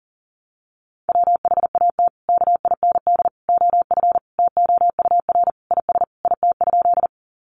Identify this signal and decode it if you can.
Morse code, heard as a keyed tone, about 40 wpm, 700 Hz. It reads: WHAT KIND OF TOUR IS IT?
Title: Sentences from Top 1400 Words 40wpm